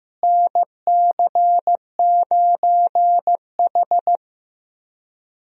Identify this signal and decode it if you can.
Morse code, heard as a keyed tone, about 16 wpm, 700 Hz. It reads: NC9H
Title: USA Call Signs - 4 Characters - Repeated 3x - 15wpm